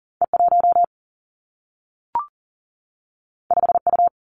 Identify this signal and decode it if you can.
Morse code, heard as a keyed tone, about 38 wpm, 700 Hz. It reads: E 1 5 U
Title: Two Challenging Letter-Number Combinations 40wpm